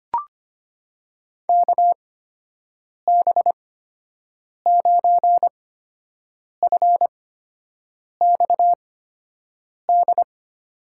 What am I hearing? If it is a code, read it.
Morse code, heard as a keyed tone, about 25 wpm, spaced as for 7 wpm, 700 Hz. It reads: KB9FXD
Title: USA Call Signs - 6 Characters - 8x Character Spacing - 25wpm